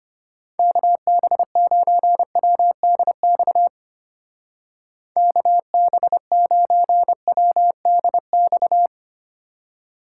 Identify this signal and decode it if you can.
Morse code, heard as a keyed tone, about 32 wpm, 700 Hz. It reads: KB9WDX KB9WDX